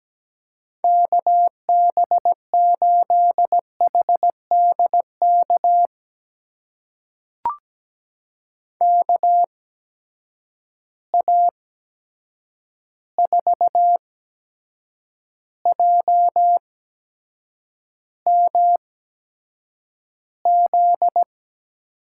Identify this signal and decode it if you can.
Morse code, heard as a keyed tone, about 17 wpm, spaced as for 7 wpm, 700 Hz. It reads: KB8HDK KA4JMZ
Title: USA Call Signs - 6 Characters - 8x Character Spacing - 17wpm